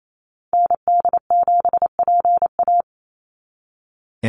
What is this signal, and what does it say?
Morse code, heard as a keyed tone, about 28 wpm, 700 Hz. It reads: ND7PA
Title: NA SOTA Activator Callsigns 28wpm